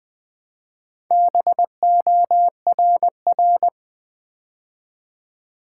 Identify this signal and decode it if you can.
Morse code, heard as a keyed tone, about 20 wpm, 700 Hz. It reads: BORR
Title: Build A Word - 2 Letters At A Time - 6-Letter Words - 20wpm